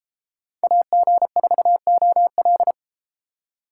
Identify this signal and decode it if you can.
Morse code, heard as a keyed tone, about 31 wpm, 700 Hz. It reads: AG4OL